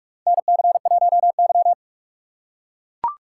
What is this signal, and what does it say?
Morse code, heard as a keyed tone, about 43 wpm, 700 Hz. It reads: NC1Y
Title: USA Call Signs - 4 Characters - 3x Character Spacing - 45wpm